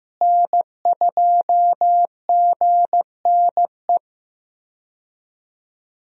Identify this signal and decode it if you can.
Morse code, heard as a keyed tone, about 15 wpm, 700 Hz. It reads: N2GNE
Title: USA Call Signs - 5 Characters - Repeated 3x - 15wpm